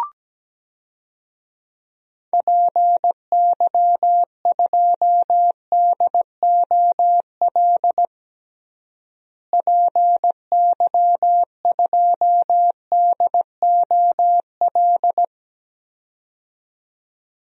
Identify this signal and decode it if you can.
Morse code, heard as a keyed tone, about 17 wpm, 700 Hz. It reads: PY2DOL PY2DOL
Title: DXCC-100 Call Signs - 6 Characters - Repeated 2x - 17wpm